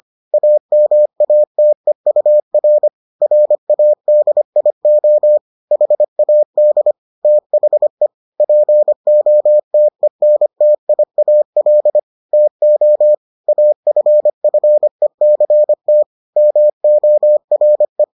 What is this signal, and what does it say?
Morse code, heard as a keyed tone, about 25 wpm, 600 Hz. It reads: AMATEUR RADIO HAD THE POTENTIAL TO AFFECT MORE